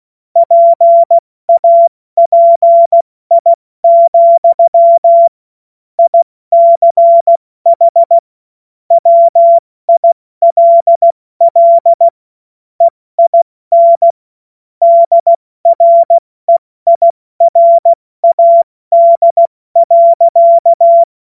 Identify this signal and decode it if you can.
Morse code, heard as a keyed tone, about 16 wpm, 680 Hz. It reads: PAPI, ICH WILL EIN DREIRAD.
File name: Witz-08-Juli-2025-70-F80